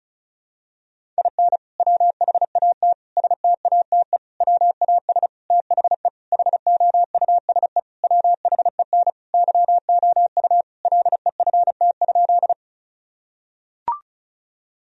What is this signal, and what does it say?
Morse code, heard as a keyed tone, about 36 wpm, 700 Hz. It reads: IN WHAT STATE WAS THE HOUSE WHEN YOU LEFT?
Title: Sentences from Top 1000 Words - Review Part 2 - 35wpm